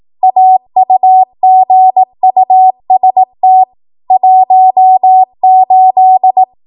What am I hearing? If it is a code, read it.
Morse code, heard as a keyed tone, about 18 wpm, 750 Hz. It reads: AUGUST 18
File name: Archive / 18 WPM / 18WPM